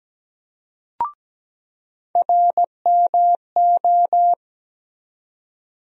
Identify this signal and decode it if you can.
Morse code, heard as a keyed tone, about 17 wpm, 700 Hz. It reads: RMO
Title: USA Call Sign Postfixes - 3 Letters 17wpm